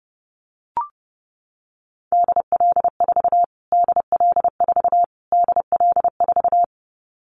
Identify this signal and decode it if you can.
Morse code, heard as a keyed tone, about 30 wpm, 700 Hz. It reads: DL4 DL4 DL4
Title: Callsign Prefixes for Top 25 DXCC - Repeated 3x - 30wpm